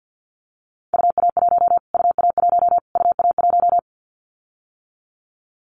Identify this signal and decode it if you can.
Morse code, heard as a keyed tone, about 50 wpm, 700 Hz. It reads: UA1 UA1 UA1
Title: Callsign Prefixes for Top 25 DXCC - Repeated 3x - 50wpm